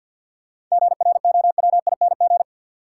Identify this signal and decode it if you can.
Morse code, heard as a keyed tone, about 47 wpm, 700 Hz. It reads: GROWING
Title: Sets of 5 Words Encore - New Word Drill - 50wpm